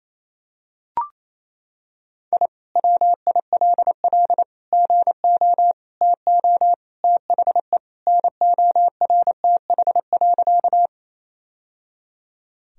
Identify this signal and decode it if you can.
Morse code, heard as a keyed tone, about 28 wpm, 700 Hz. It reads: I WILL GO TO THE NORTH.
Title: Sentences from Top 1000 Words - Review Part 1 - 28wpm